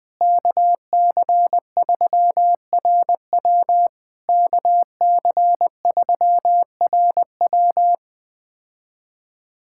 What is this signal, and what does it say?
Morse code, heard as a keyed tone, about 20 wpm, 700 Hz. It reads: KC3RW KC3RW
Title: POTA QSO Practice - 20wpm